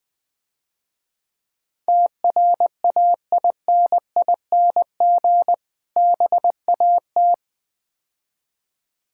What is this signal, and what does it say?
Morse code, heard as a keyed tone, about 22 wpm, 700 Hz. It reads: TRAINING BAT
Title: Difficult Words - Speed-Racing with Hints - 20wpm